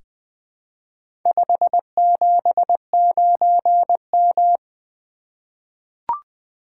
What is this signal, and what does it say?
Morse code, heard as a keyed tone, about 20 wpm, 700 Hz. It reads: 579M
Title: Four Letter-Number Combinations - 20wpm